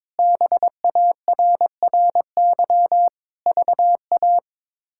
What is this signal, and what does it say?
Morse code, heard as a keyed tone, about 22 wpm, 700 Hz. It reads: BARRY VA
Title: K1USN SST QSO Practice - 22wpm